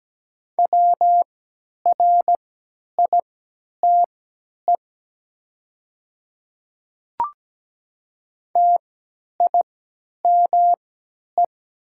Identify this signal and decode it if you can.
Morse code, heard as a keyed tone, about 17 wpm, spaced as for 10 wpm, 700 Hz. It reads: WRITE TIME
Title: 3x Character Spacing - Top 100 Words - 17wpm